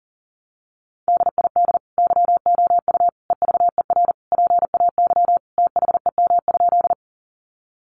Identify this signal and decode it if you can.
Morse code, heard as a keyed tone, about 40 wpm, 700 Hz. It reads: DID YOU EVER PAY THEM?
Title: Sentences from Top 700 Words 40wpm